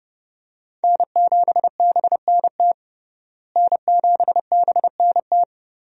Morse code, heard as a keyed tone, about 32 wpm, 700 Hz. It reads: N7BNT N7BNT
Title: USA Call Signs - 5 Characters - Repeated 2x - 30wpm